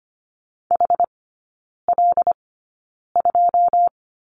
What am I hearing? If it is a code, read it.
Morse code, heard as a keyed tone, about 25 wpm, spaced as for 9 wpm, 690 Hz. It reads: HL2
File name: Lesson 14e - Number 2 - Three Characters 25wpm